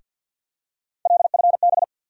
Morse code, heard as a keyed tone, about 54 wpm, 700 Hz. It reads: LFB
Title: USA Call Sign Postfixes - 3 Letters 50wpm